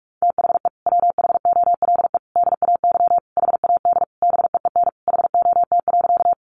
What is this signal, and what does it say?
Morse code, heard as a keyed tone, about 44 wpm, 700 Hz. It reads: THE WHOLE DAY HAD BEEN HOT.
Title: Sentences from Top 500 Words 45wpm